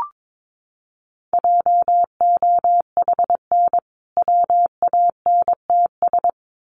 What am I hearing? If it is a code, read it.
Morse code, heard as a keyed tone, about 22 wpm, 700 Hz. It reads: JOHN WANTS
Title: Additional Easy Phrases 22wpm